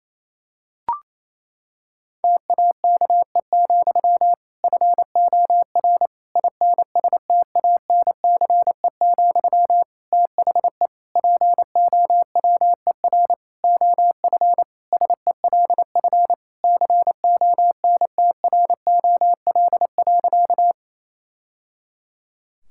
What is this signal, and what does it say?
Morse code, heard as a keyed tone, about 27 wpm, 700 Hz. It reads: TAKE, FOR INSTANCE, THE POWER OF SELF CONTROL.
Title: Sentences from Top 1000 Words - Review Part 5 - 28wpm